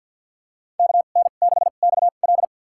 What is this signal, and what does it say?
Morse code, heard as a keyed tone, about 47 wpm, spaced as for 35 wpm, 700 Hz. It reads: KN6XL
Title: USA Call Signs - 5 Characters - 2x Character Spacing - 50wpm